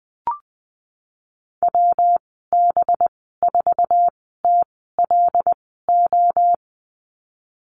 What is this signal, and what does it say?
Morse code, heard as a keyed tone, about 20 wpm, spaced as for 15 wpm, 700 Hz. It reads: WB4TLO
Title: 186 Lesson 30g - Letter b - Call Signs 20wpm